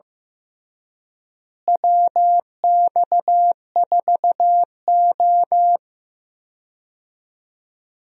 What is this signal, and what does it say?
Morse code, heard as a keyed tone, about 15 wpm, 700 Hz. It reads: WX4O
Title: USA Call Signs - 4 Characters - Repeated 2x - 15wpm